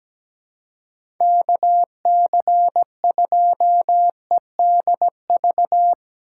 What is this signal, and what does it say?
Morse code, heard as a keyed tone, about 17 wpm, 700 Hz. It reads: K C 2 E D V
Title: USA Call Signs - 6 Characters - 6x Character Spacing - 17wpm